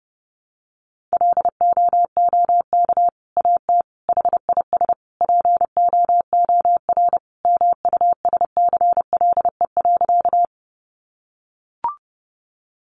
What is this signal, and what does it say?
Morse code, heard as a keyed tone, about 30 wpm, 700 Hz. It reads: LOOK AT HIS POOR MUSCLE.
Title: Sentences from Top 1300 Words 30wpm